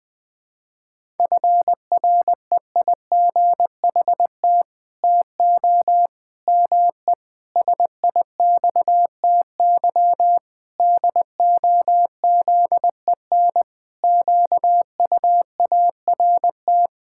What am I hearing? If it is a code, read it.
Morse code, heard as a keyed tone, about 20 wpm, 700 Hz. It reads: FREIGHT TO ME SIXTY DOZEN QUART
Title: Pangram Challenge v2 - 20wpm